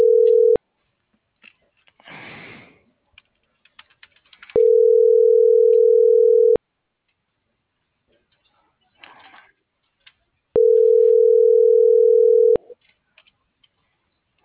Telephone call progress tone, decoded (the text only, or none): ringback tone